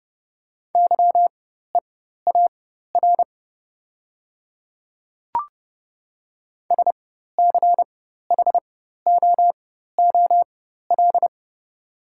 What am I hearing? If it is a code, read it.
Morse code, heard as a keyed tone, about 30 wpm, spaced as for 14 wpm, 700 Hz. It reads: YEAR SCHOOL
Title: 4x Character Spacing - Top 300 Words - 30wpm